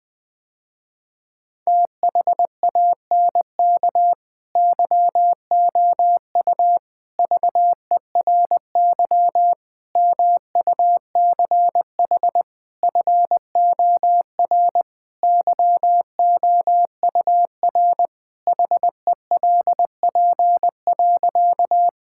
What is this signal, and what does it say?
Morse code, heard as a keyed tone, about 20 wpm, 700 Hz. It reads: THANK YOU VERY MUCH FOR YOUR HELP.
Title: Sentences with 4-Word Phrases - Spoken Beginning - 20wpm